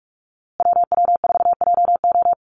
Morse code, heard as a keyed tone, about 44 wpm, 700 Hz. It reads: WW4JO
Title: USA Call Signs Repeated 2x 45wpm